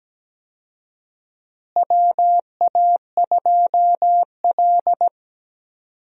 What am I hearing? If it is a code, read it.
Morse code, heard as a keyed tone, about 17 wpm, 700 Hz. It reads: WA2L
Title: USA Call Signs - 4 Characters - Repeated 2x - 17wpm